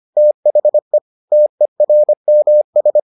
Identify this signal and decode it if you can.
Morse code, heard as a keyed tone, about 25 wpm, 600 Hz. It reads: THE TERMS